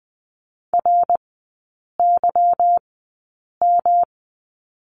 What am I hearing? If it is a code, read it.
Morse code, heard as a keyed tone, about 20 wpm, spaced as for 8 wpm, 690 Hz. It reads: RYM
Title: 145 Lesson 24e - Letter y - Three Characters 20wpm